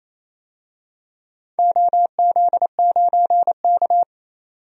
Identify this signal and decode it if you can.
Morse code, heard as a keyed tone, about 28 wpm, 700 Hz. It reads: OZ9K
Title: DXCC-100 Call Signs - 4 Characters - 28wpm